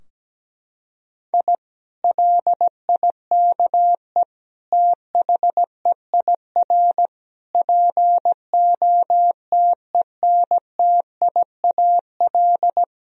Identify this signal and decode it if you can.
Morse code, heard as a keyed tone, about 17 wpm, 700 Hz. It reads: I LIKE THEIR POTENTIAL
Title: Sentences from Top 1000 Words - Review Part 4 - 17wpm